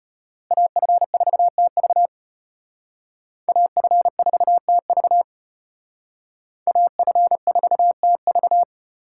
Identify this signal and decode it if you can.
Morse code, heard as a keyed tone, about 29 wpm, 700 Hz. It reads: AF4TV AF4TV AF4TV
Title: Speed Racing - USA 2x2 Call Signs 25wpm